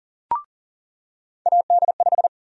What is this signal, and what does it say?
Morse code, heard as a keyed tone, about 40 wpm, 700 Hz. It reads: AD5
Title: USA Call Sign Prefixes - 40wpm